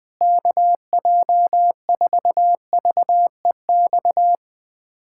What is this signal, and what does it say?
Morse code, heard as a keyed tone, about 20 wpm, 700 Hz. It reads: KJ4VEX
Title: USA Callsign DE Callsign - 6 Characters - 20wpm